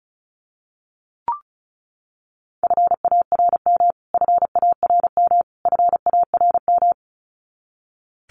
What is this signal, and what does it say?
Morse code, heard as a keyed tone, about 34 wpm, 700 Hz. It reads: FARM FARM FARM
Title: Top 500 Words Repeated 35wpm